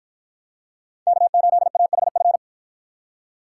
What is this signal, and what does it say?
Morse code, heard as a keyed tone, about 53 wpm, 700 Hz. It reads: K8AHF